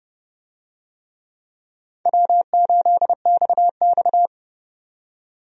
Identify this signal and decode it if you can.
Morse code, heard as a keyed tone, about 29 wpm, 700 Hz. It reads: W8XX
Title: USA Call Signs - 4 Characters - 30wpm